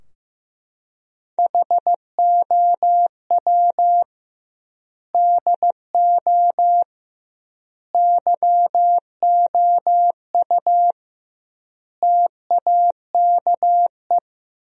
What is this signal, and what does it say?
Morse code, heard as a keyed tone, about 15 wpm, 700 Hz. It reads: HOW DO YOU TAKE
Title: 2x Word-Spacing - Sentences from Top 200 Words - 15wpm